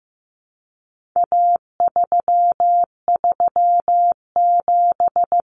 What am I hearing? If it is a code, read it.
Morse code, heard as a keyed tone, about 15 wpm, 700 Hz. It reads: A337
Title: Common Cut Numbers - 4-Digit 15wpm